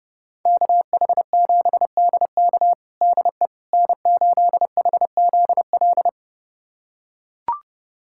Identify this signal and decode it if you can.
Morse code, heard as a keyed tone, about 32 wpm, 700 Hz. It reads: KH7DK DE N8HZL
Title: USA Callsign DE Callsign - 5 Characters - 30wpm